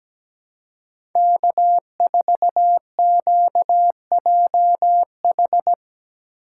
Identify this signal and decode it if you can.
Morse code, heard as a keyed tone, about 17 wpm, 700 Hz. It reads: K4QJH
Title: USA Call Signs - 2x Character Spacing - 17wpm